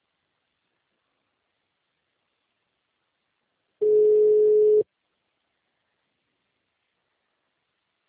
Ringback tone (Europe, 425 Hz).